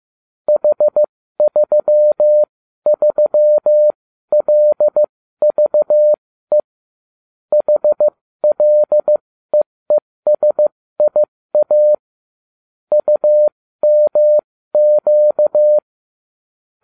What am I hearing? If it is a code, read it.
Morse code, heard as a keyed tone, about 15 wpm, spaced as for 13 wpm, 600 Hz. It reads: H33LVE HLEESIA UMQ